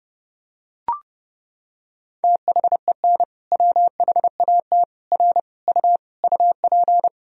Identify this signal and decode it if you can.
Morse code, heard as a keyed tone, about 32 wpm, 700 Hz. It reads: THEN WHAT R U UP
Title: Sentences from Top 100 Words and Abbreviations 30wpm